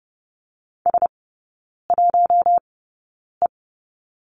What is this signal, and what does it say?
Morse code, heard as a keyed tone, about 30 wpm, spaced as for 9 wpm, 700 Hz. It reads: S1E
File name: Lesson 16e - Letter c - Three Characters 30wpm